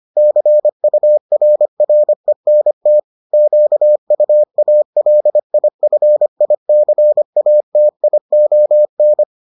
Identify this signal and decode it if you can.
Morse code, heard as a keyed tone, about 25 wpm, 600 Hz. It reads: CURRENT QUALIFICATION